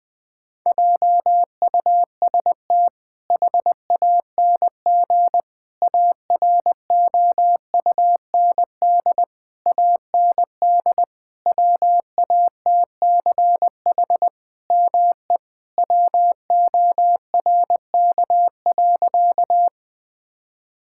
Morse code, heard as a keyed tone, about 20 wpm, 700 Hz. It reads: JUST HANG AROUND AND WATCH ME WORK.
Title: Sentences from Top 1600 Words 20wpm